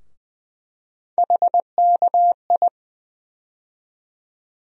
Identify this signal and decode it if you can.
Morse code, heard as a keyed tone, about 20 wpm, 700 Hz. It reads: HKI
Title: USA Call Sign Postfixes - 3 Letters 20wpm